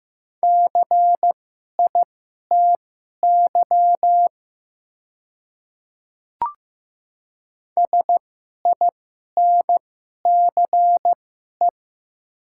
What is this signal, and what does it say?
Morse code, heard as a keyed tone, about 15 wpm, spaced as for 11 wpm, 700 Hz. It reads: CITY SINCE
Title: 2x Character Spacing - Top 300 Words - 15wpm